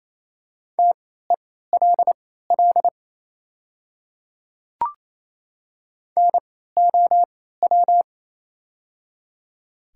Morse code, heard as a keyed tone, about 27 wpm, spaced as for 16 wpm, 700 Hz. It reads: TELL NOW
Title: 3x Character Spacing - Top 200 Words - 28wpm